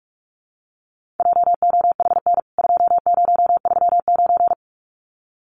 Morse code, heard as a keyed tone, about 44 wpm, 700 Hz. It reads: JOHN 2039
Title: CWT Exchange 45wpm